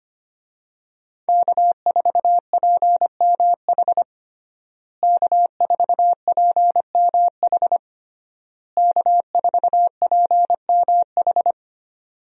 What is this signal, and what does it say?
Morse code, heard as a keyed tone, about 25 wpm, 700 Hz. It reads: K4PMH K4PMH K4PMH
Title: USA Call Signs - 5 Characters - Repeated 3x - 25wpm